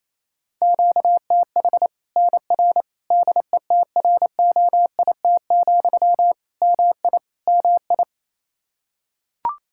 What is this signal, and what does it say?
Morse code, heard as a keyed tone, about 27 wpm, 700 Hz. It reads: QTH NR DETROIT, MI MI
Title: Practice QSO - Exchange 1 - Beginning - 28wpm